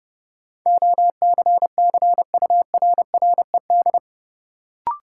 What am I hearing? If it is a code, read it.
Morse code, heard as a keyed tone, about 32 wpm, 700 Hz. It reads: OCCURRED
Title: Sets of 5 Words Encore - New Word Drill - 30wpm